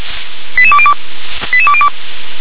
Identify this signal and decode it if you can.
Five-tone selective call (ZVEI), heard as quick runs of five tones, tones of about 70 ms. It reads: 88202 88202
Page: SDR/ZVEI